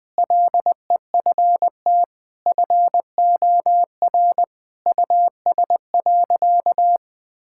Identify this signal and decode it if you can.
Morse code, heard as a keyed tone, about 20 wpm, 700 Hz. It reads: LEFT FOR US.
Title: Sentences from Top 1000 Words - Review Part 5 - 20wpm